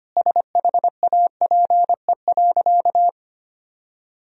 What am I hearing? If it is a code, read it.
Morse code, heard as a keyed tone, about 25 wpm, 700 Hz. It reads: SHAPE.
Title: Sentences from Top 1000 Words - Review Part 3 - 25wpm